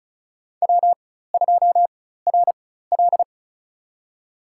Morse code, heard as a keyed tone, about 36 wpm, spaced as for 16 wpm, 700 Hz. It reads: W2RL
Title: USA Call Signs - 4 Characters - 4x Character Spacing - 35wpm